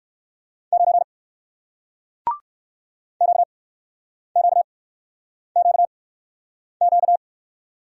Morse code, heard as a keyed tone, about 56 wpm, 700 Hz. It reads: 9 Q Q Q Q
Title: Speed-Racing Single Letter-Number 45wpm